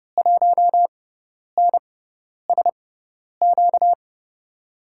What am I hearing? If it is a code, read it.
Morse code, heard as a keyed tone, about 31 wpm, spaced as for 10 wpm, 700 Hz. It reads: 1NSQ
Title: USA Call Signs - 6 Characters - 6x Character Spacing - 30wpm